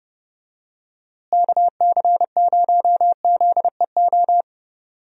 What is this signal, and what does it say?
Morse code, heard as a keyed tone, about 31 wpm, 700 Hz. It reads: KC0ZEO
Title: USA Call Signs - 6 Characters - Repeated 2x - 30wpm